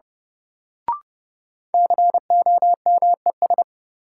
Morse code, heard as a keyed tone, about 30 wpm, 700 Hz. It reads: COMES
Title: Sentences from Top 1000 Words - Review Part 5 - New Word Drill - 30wpm